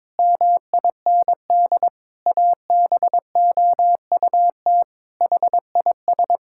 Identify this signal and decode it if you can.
Morse code, heard as a keyed tone, about 22 wpm, 700 Hz. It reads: MIND ABOUT HIS
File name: Sentences with 7-Word Phrases - 22wpm